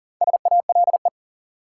Morse code, heard as a keyed tone, about 40 wpm, 700 Hz. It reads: SALE
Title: Sentences from Top 1400 Words - New Word Drill Alt - 40wpm